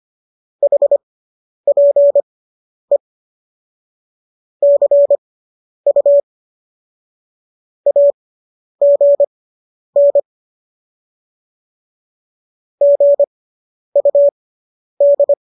Morse code, heard as a keyed tone, about 25 wpm, spaced as for 10 wpm, 570 Hz. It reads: HPE CU AGN GUD